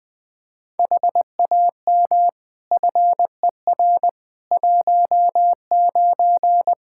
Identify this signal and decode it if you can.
Morse code, heard as a keyed tone, about 20 wpm, 700 Hz. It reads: HAM FER 19
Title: K1USN SST QSO Practice 20wpm